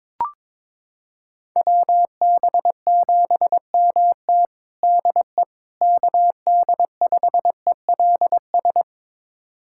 Morse code, heard as a keyed tone, about 22 wpm, 700 Hz. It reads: WB7MT DE KD5ELS
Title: USA Callsign DE Callsign 22wpm